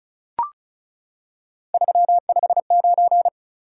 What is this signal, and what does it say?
Morse code, heard as a keyed tone, about 36 wpm, 700 Hz. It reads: 359
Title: Top 180 QSO Elements 35wpm